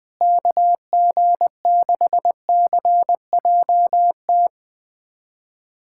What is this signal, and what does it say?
Morse code, heard as a keyed tone, about 21 wpm, 700 Hz. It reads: KG6CJT
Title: USA Callsign DE Callsign 20wpm